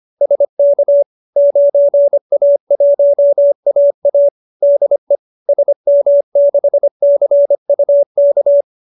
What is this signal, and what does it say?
Morse code, heard as a keyed tone, about 25 wpm, 570 Hz. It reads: SK 9A1AA DE SM6CUK